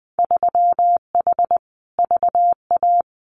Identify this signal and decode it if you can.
Morse code, heard as a keyed tone, about 20 wpm, 700 Hz. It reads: 3H VA
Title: Winter Field Day Exchanges 20wpm